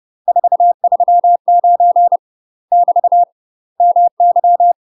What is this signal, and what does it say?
Morse code, heard as a keyed tone, about 30 wpm, 700 Hz. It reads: 439 <BT> MY
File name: UK MASTER QSOs TRAINING / 30WPM / UK MASTER QSOs 0007